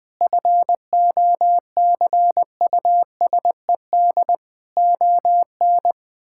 Morse code, heard as a keyed tone, about 21 wpm, 700 Hz. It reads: FOCUSED ON
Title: Sentences with 7-Word Phrases - 20wpm